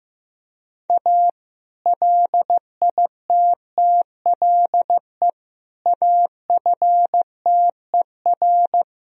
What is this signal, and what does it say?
Morse code, heard as a keyed tone, about 16 wpm, 700 Hz. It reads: A LITTLE AFTER
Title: Sentences from Top 1000 Words - Review Part 2 - 15wpm